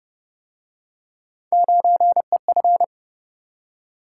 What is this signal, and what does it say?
Morse code, heard as a keyed tone, about 32 wpm, 700 Hz. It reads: 9EF